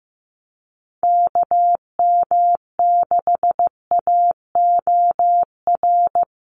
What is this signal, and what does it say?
Morse code, heard as a keyed tone, about 15 wpm, 700 Hz. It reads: KM6AOR
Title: USA Call Signs 15wpm